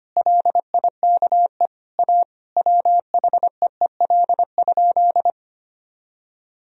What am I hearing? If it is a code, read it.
Morse code, heard as a keyed tone, about 25 wpm, 700 Hz. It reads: LIKE A WHEEL?
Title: Sentences from Top 1000 Words - Review Part 3 - 25wpm